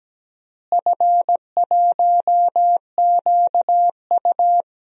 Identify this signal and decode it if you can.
Morse code, heard as a keyed tone, about 17 wpm, 700 Hz. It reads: F1QU
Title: DXCC-100 Call Signs - 4 Characters - Repeated 2x - 17wpm